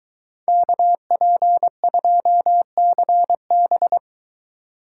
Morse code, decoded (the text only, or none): KP2CB